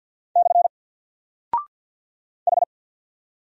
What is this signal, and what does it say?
Morse code, heard as a keyed tone, about 48 wpm, 700 Hz. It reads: / H
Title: Single Character (Letter:Number:Punctuation:Prosign) Alt 50wpm